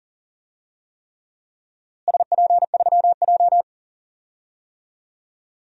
Morse code, heard as a keyed tone, about 37 wpm, 700 Hz. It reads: SP3J
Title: DXCC-100 Call Signs - 4 Characters - Repeated 3x - 40wpm